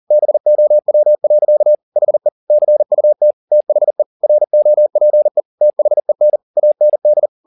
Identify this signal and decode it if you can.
Morse code, heard as a keyed tone, about 40 wpm, 600 Hz. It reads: BOW. HE CUT THE ROPE THEN AND
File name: the old man and the sea 2 40wpm